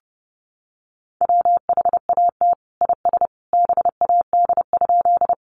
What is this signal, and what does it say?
Morse code, heard as a keyed tone, about 30 wpm, 700 Hz. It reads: WHAT IS BAD?
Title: Sentences from Top 700 Words 30wpm